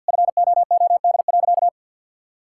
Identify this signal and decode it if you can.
Morse code, heard as a keyed tone, about 51 wpm, 690 Hz. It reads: FOOD.
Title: Sentences from Top 1000 Words - Review Part 5 - 50wpm